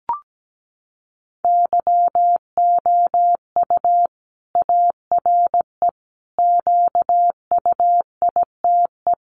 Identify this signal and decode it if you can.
Morse code, heard as a keyed tone, about 17 wpm, 700 Hz. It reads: YOU ARE QUITE
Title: Sentences from Top 700 Words 17wpm